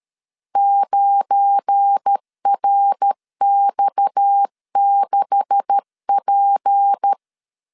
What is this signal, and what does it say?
Morse code, heard as a keyed tone, about 13 wpm, 800 Hz. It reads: 9RX6P
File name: LEZIONI / 10 SIGNS / 10D COMA 12wpm 2544